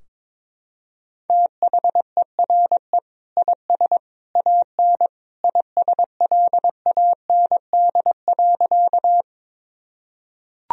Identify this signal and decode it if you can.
Morse code, heard as a keyed tone, about 22 wpm, 700 Hz. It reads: THERE IS AN ISLAND.
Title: Sentences from Top 1000 Words - Review Part 3 - 22wpm